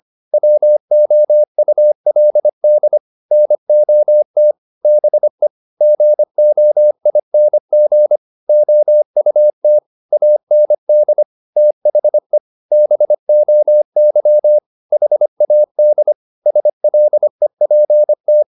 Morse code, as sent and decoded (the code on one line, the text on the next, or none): .-- --- ..- .-.. -.. / -. --- - / -... / --. --- .. -. --. / --- ..- - / .- -. -.. / - .... / -... --- -.-- / .... .- -.. / ... .-.. .--. -
WOULD NOT BE GOING OUT AND THE BOY HAD SLEPT